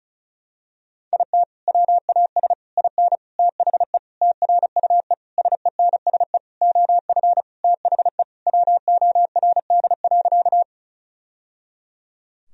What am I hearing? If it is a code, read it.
Morse code, heard as a keyed tone, about 36 wpm, 700 Hz. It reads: IT WAS IN THE TRUE SENSE OF THE WORD.
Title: Sentences from Top 1000 Words - Review Part 4 - 35wpm